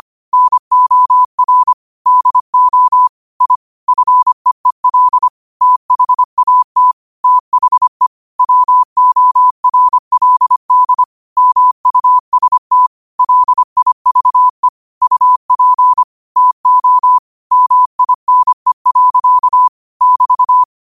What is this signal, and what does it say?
Morse code, heard as a keyed tone, about 25 wpm, 1000 Hz. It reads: NOR DO I FEEL THAT THE WORLD MUST LIVE UP TO MINE. <BT>
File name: qotdfiles / 25wpm